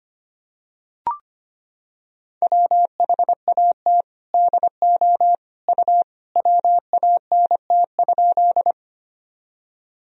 Morse code, heard as a keyed tone, about 25 wpm, 700 Hz. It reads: WHAT DO U WANT?
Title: Sentences from Top 200 Words and Abbreviations 25wpm